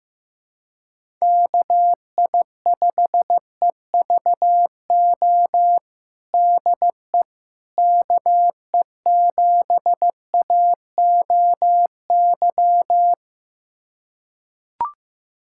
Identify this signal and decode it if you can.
Morse code, heard as a keyed tone, about 15 wpm, 700 Hz. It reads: KI5EVO DE KE7AOY
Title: USA Callsign DE Callsign 15wpm